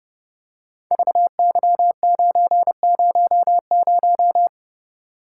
Morse code, heard as a keyed tone, about 30 wpm, 700 Hz. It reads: VY900
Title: Code Groups - 5 Letter:Numbers - 30wpm